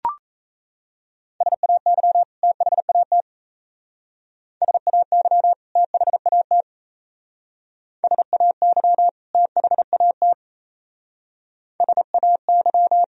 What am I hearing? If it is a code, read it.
Morse code, heard as a keyed tone, about 32 wpm, 700 Hz. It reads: SAY THAT SAY THAT SAY THAT SAY
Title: Speed-Racing - Sets of 2 Words 28wpm